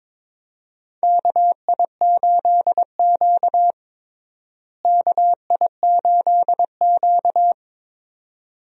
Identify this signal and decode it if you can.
Morse code, heard as a keyed tone, about 21 wpm, 700 Hz. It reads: KI8Q KI8Q
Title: USA Call Signs - 4 Characters - Repeated 2x - 22wpm